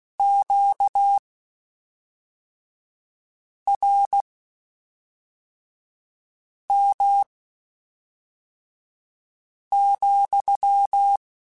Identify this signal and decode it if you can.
Morse code, heard as a keyed tone, about 16 wpm, spaced as for 3 wpm, 800 Hz. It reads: QRM,
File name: Q Calls / Q Calls-mp3s / Q Calls sppc @5